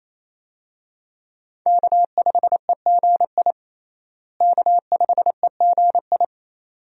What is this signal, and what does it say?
Morse code, heard as a keyed tone, about 28 wpm, 700 Hz. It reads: K5EGI K5EGI
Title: USA Call Signs - 5 Characters - Repeated 2x - 28wpm